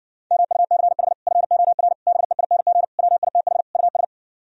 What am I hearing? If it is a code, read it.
Morse code, heard as a keyed tone, about 60 wpm, 700 Hz. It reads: MUCH FOR BEING WITH US